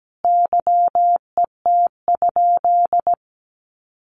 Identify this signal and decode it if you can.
Morse code, heard as a keyed tone, about 17 wpm, 700 Hz. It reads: YET?
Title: Sentences from Top 1300 Words 17wpm